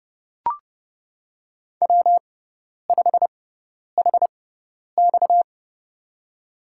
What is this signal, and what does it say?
Morse code, heard as a keyed tone, about 30 wpm, spaced as for 10 wpm, 700 Hz. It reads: W5HX
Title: USA Call Signs - 4 Characters - 6x Character Spacing - 30wpm